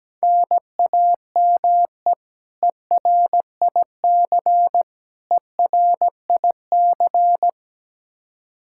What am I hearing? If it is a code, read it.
Morse code, heard as a keyed tone, about 17 wpm, 700 Hz. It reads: NAME ERIC ERIC
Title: Practice QSO - Exchange 1 - Beginning - 17wpm